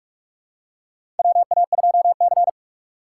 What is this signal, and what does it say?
Morse code, heard as a keyed tone, about 46 wpm, 700 Hz. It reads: WA2C
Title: USA Call Signs - 4 Characters - 3x Character Spacing - 45wpm